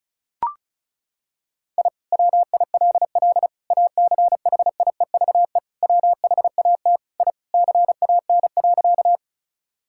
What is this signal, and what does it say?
Morse code, heard as a keyed tone, about 34 wpm, 700 Hz. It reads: I WILL ACHIEVE WHAT I CAN.
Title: Sentences from Top 1000 Words - Review Part 6 - 35wpm